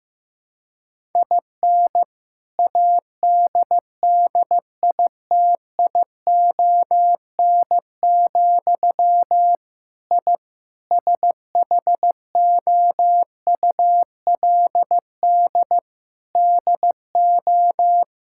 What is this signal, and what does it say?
Morse code, heard as a keyed tone, about 16 wpm, 700 Hz. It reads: IN ADDITION, I SHOULD DO
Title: Sentences from Top 1000 Words - Review Part 6 - 15wpm